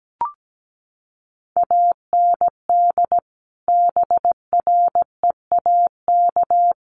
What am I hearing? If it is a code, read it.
Morse code, heard as a keyed tone, about 17 wpm, 700 Hz. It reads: AND BREAK
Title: Sentences from Top 900 Words 17wpm